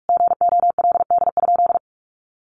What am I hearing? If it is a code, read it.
Morse code, heard as a keyed tone, about 44 wpm, 700 Hz. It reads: GOLD?
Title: Sentences from Top 1500 Words 45wpm